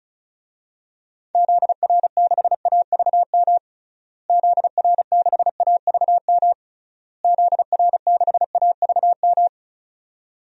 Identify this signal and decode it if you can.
Morse code, heard as a keyed tone, about 34 wpm, 700 Hz. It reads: ZR6AVM ZR6AVM ZR6AVM
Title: DXCC-100 Call Signs - 6 Characters - Repeated 3x - 35wpm